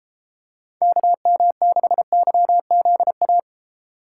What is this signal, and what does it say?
Morse code, heard as a keyed tone, about 33 wpm, 700 Hz. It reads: KM6YZA